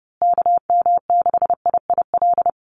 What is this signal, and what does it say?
Morse code, heard as a keyed tone, about 30 wpm, 700 Hz. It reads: KM6IIL
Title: USA Call Signs Repeated 2x 30wpm